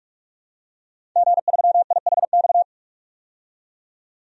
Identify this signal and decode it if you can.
Morse code, heard as a keyed tone, about 41 wpm, 700 Hz. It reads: G3IHX
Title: DXCC-100 Call Signs - 5 Characters - 45wpm